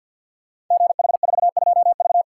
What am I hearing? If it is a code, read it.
Morse code, heard as a keyed tone, about 52 wpm, 700 Hz. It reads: GH42V
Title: Code Groups - 5 Letter:Numbers - 50wpm